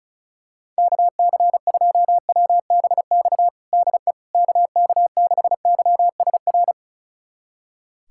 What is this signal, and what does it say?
Morse code, heard as a keyed tone, about 36 wpm, 700 Hz. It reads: KC2WBX DE KK6YSR
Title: USA Callsign DE Callsign - 6 Characters - 35wpm